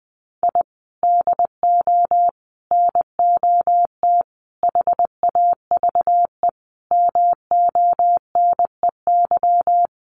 Morse code, heard as a keyed tone, about 20 wpm, 700 Hz. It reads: I DO NOT HAVE MONEY
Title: Sentences from Top 1400 Words 20wpm